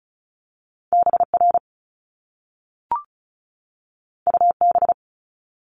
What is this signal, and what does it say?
Morse code, heard as a keyed tone, about 36 wpm, 700 Hz. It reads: BR UB
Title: Top Two Letter Combinations in Words 35wpm